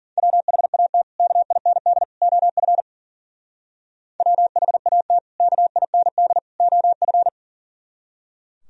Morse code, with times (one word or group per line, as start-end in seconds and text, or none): 0.2-1.0 WHAT
1.2-2.0 KIND
2.2-2.8 OF
4.2-5.2 WHAT
5.4-6.4 KIND
6.6-7.3 OF